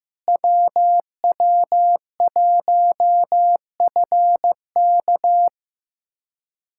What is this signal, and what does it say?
Morse code, heard as a keyed tone, about 16 wpm, 700 Hz. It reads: WW1FK
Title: USA Callsign DE Callsign - 5 Characters - 15wpm